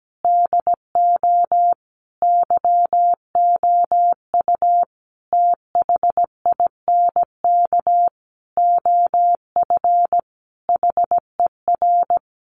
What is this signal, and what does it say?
Morse code, heard as a keyed tone, about 17 wpm, 700 Hz. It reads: DO YOU THINK OF HER
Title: Sentences from Top 1100 Words 17wpm